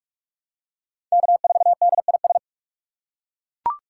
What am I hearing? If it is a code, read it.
Morse code, heard as a keyed tone, about 41 wpm, 700 Hz. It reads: K4DIS
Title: USA Call Signs - 8x Character Spacing - 45wpm